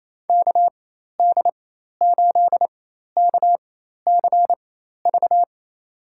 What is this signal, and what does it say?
Morse code, heard as a keyed tone, about 28 wpm, spaced as for 13 wpm, 700 Hz. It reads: KD8KCV
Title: USA Call Signs - 4x Character Spacing - 28wpm